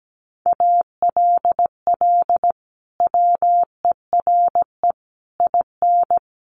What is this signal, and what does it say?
Morse code, heard as a keyed tone, about 17 wpm, 700 Hz. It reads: ALL WERE IN
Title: Sentences from Top 1100 Words 17wpm